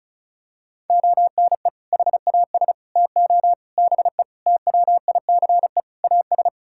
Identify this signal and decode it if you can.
Morse code, heard as a keyed tone, about 36 wpm, 700 Hz. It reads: ONE HAS TO BE TWICE AS
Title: Sentences from Top 1600 Words 35wpm